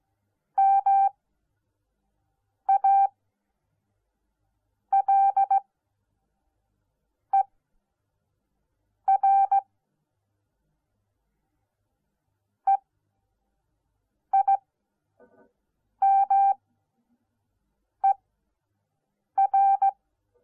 Morse code, heard as a keyed tone, about 17 wpm, spaced as for 5 wpm, 780 Hz. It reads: MALER EIMER